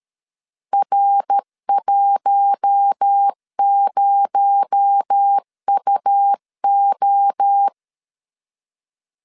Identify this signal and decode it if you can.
Morse code, heard as a keyed tone, about 13 wpm, 800 Hz. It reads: R10UO
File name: LEZIONI / 6 PKX0 / 6 PKX0 12wpm 2231